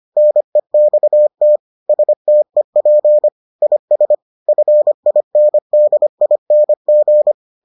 Morse code, heard as a keyed tone, about 25 wpm, 600 Hz. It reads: NEXT STEP IS FINDING